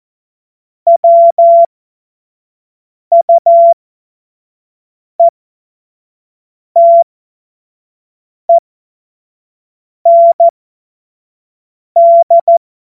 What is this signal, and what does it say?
Morse code, heard as a keyed tone, about 14 wpm, spaced as for 5 wpm, 680 Hz. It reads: WUETEND
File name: Morsekurs / Witze / Witz 01. März 2023 25 F70